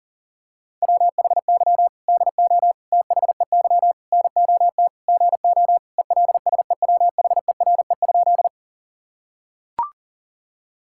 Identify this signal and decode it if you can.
Morse code, heard as a keyed tone, about 37 wpm, 700 Hz. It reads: WHY DO THEY NOT GO ELSEWHERE?
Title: Sentences from Top 1600 Words 40wpm